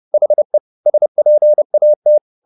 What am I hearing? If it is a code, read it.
Morse code, heard as a keyed tone, about 30 wpm, 600 Hz. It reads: HE SPAT